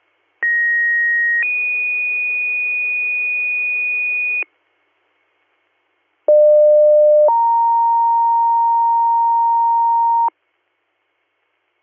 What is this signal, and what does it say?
Two-tone page, heard as a one-second tone, then a three-second tone, about 1900 then 2450 Hz, about 600 then 930 Hz. It is heard from 0.4 to 10.3 s.